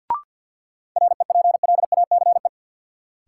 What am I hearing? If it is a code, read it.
Morse code, heard as a keyed tone, about 53 wpm, 700 Hz. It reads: REPLACE
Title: Sentences from Top 1400 Words - New Word Drill Alt - 50wpm